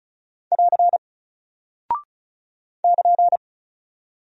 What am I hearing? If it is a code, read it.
Morse code, heard as a keyed tone, about 36 wpm, 700 Hz. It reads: <AR> <KN>
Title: Single Character (Letter:Number:Punctuation:Prosign) 35wpm